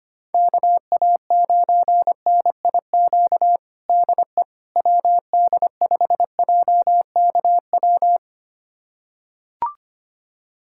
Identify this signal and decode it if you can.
Morse code, heard as a keyed tone, about 25 wpm, 700 Hz. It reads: KA9NIQ DE WD5JKW
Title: USA Callsign DE Callsign - 6 Characters - 25wpm